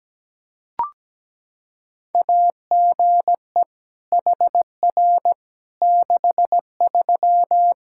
Morse code, heard as a keyed tone, about 17 wpm, 700 Hz. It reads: AGE HR 63
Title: K1USN SST QSO Practice 17wpm